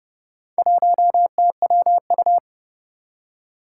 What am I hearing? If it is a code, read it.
Morse code, heard as a keyed tone, about 30 wpm, 700 Hz. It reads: 1TWU
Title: Four Challenging Letter-Number Combinations 30wpm